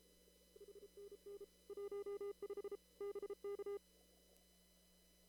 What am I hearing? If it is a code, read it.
Morse code, heard as a keyed tone, about 33 wpm, 400 Hz. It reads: HNN 15 BK
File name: CW / kt5j wpx cw-op-k5pi